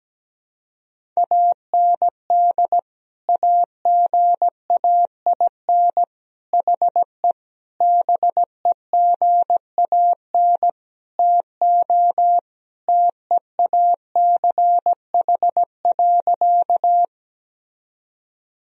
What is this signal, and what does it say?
Morse code, heard as a keyed tone, about 17 wpm, 700 Hz. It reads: AND AGAIN HE BEGAN TO TEACH.
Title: Sentences from Top 1000 Words - Review Part 2 - 17wpm